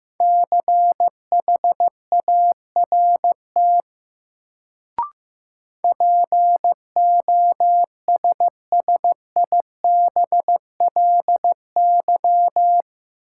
Morse code, heard as a keyed tone, about 15 wpm, 700 Hz. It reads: CHART POSSIBLY
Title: Sentences from Top 1300 Words - New Word Drill Alt - 15wpm